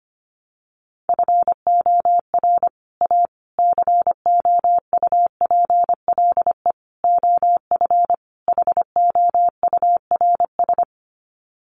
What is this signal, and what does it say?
Morse code, heard as a keyed tone, about 25 wpm, 700 Hz. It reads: FOR A COUPLE OF HOURS
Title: Sets of 5 Words Encore2 25wpm